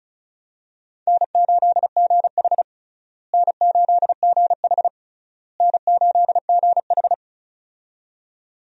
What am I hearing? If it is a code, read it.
Morse code, heard as a keyed tone, about 36 wpm, 700 Hz. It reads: N8GH N8GH N8GH
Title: USA Call Signs - 4 Characters - Repeated 3x - 35wpm